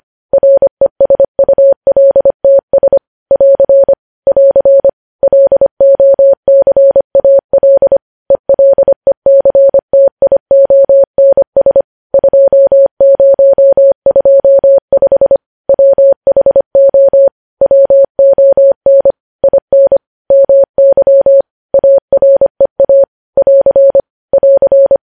Morse code, heard as a keyed tone, about 25 wpm, 550 Hz. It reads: RESULTS <AR> <AR> LOCAL ELECTIONS 2025 WHO WON IN MY AREA <AR> <AR>
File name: podcast / News headlines-1x-25x25